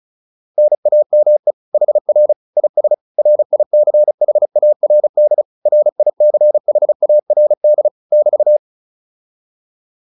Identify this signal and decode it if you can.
Morse code, heard as a keyed tone, about 36 wpm, 600 Hz. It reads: NAME HR IS RICHARD RICHARD <BT>